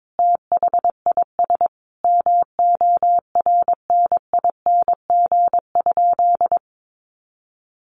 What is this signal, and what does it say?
Morse code, heard as a keyed tone, about 22 wpm, 700 Hz. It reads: THIS MORNING?
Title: Sentences from Top 1200 Words 22wpm